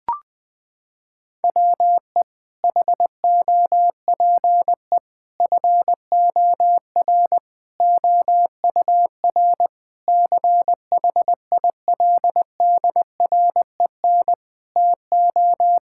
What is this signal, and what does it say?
Morse code, heard as a keyed tone, about 21 wpm, 700 Hz. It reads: WE HOPE FOR OUR CHILDREN TO
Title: Sentences from Top 1000 Words - Review Part 6 - 20wpm